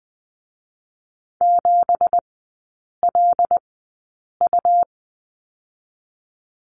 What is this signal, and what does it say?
Morse code, heard as a keyed tone, about 20 wpm, spaced as for 8 wpm, 700 Hz. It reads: 7LU
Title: 218 Lesson 35e - Number 0 - Three Characters 20wpm